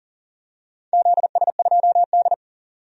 Morse code, heard as a keyed tone, about 40 wpm, 700 Hz. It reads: ZS2D
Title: DXCC-100 Call Signs - 4 Characters - 40wpm